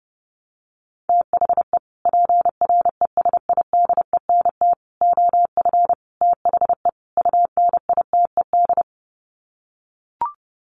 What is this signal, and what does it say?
Morse code, heard as a keyed tone, about 30 wpm, 700 Hz. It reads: THE PRESIDENT OF THE UNITED
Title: Sets of 5 Words 30wpm